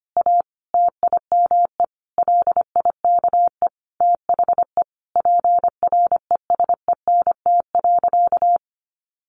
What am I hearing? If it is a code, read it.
Morse code, heard as a keyed tone, about 25 wpm, 700 Hz. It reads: A TIME LIKE THE PRESENT.
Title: Sentences from Top 900 Words 25wpm